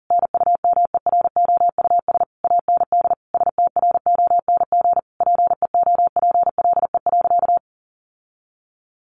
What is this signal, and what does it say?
Morse code, heard as a keyed tone, about 37 wpm, 700 Hz. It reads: NUMEROUS AND STRONG PEOPLE.
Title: Sentences from Top 1400 Words 40wpm